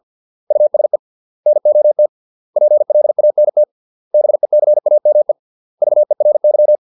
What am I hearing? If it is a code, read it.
Morse code, heard as a keyed tone, about 50 wpm, 600 Hz. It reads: USE NOT PLANT BECAME VERY